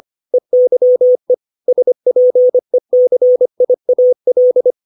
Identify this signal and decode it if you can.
Morse code, heard as a keyed tone, about 25 wpm, 500 Hz. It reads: EYE SPECIAL
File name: 632 words / 632 Words 25WPM